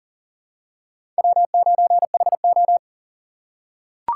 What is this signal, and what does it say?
Morse code, heard as a keyed tone, about 40 wpm, 700 Hz. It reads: W9HO
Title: USA Call Signs - 4 Characters - 2x Character Spacing - 40wpm